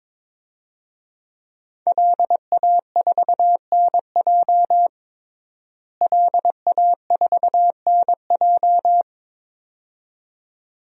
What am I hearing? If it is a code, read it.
Morse code, heard as a keyed tone, about 21 wpm, 700 Hz. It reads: LA4NJ LA4NJ
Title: DXCC-100 Call Signs - 5 Characters - Repeated 2x - 22wpm